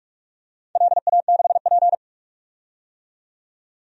Morse code, heard as a keyed tone, about 46 wpm, 700 Hz. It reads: LA6P
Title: DXCC-100 Call Signs - 4 Characters - 45wpm